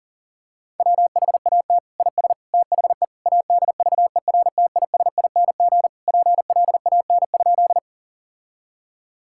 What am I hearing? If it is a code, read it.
Morse code, heard as a keyed tone, about 40 wpm, 700 Hz. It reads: WHAT IS THE ADVERTISING PLAN?
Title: Sentences from Top 1000 Words - Review Part 6 - 40wpm